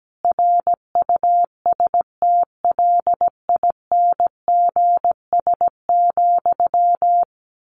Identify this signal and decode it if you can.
Morse code, heard as a keyed tone, about 17 wpm, 700 Hz. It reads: RUSTLINGS,
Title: A Sound Of Thunder 17wpm